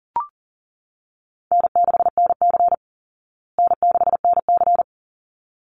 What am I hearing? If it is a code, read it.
Morse code, heard as a keyed tone, about 40 wpm, 700 Hz. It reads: N6NC N6NC
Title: Call Signs Repeated 3x - 40wpm